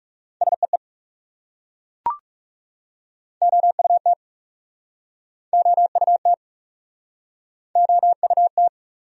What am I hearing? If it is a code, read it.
Morse code, heard as a keyed tone, about 39 wpm, 700 Hz. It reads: SEE OUT OUT OUT